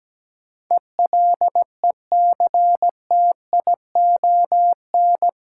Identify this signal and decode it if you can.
Morse code, heard as a keyed tone, about 17 wpm, 700 Hz. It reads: ELECTION
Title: Sentences from Top 1500 Words - New Word Drill Alt - 17wpm